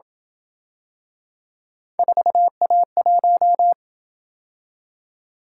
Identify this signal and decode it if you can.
Morse code, heard as a keyed tone, about 27 wpm, 700 Hz. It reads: 4A1